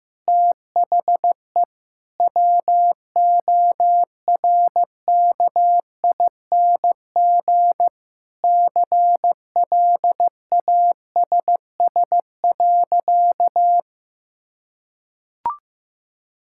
Morse code, heard as a keyed tone, about 16 wpm, 700 Hz. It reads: THE WORKING CLASS.
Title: Sentences from Top 1000 Words - Review Part 6 - 15wpm